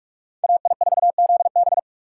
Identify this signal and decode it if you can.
Morse code, heard as a keyed tone, about 59 wpm, spaced as for 51 wpm, 700 Hz. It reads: AI4ZB
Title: Speed Racing - USA 2x2 Call Signs 45wpm